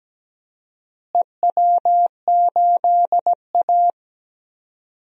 Morse code, heard as a keyed tone, about 17 wpm, 700 Hz. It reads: EW8A
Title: DXCC-100 Call Signs - 4 Characters - Repeated 2x - 17wpm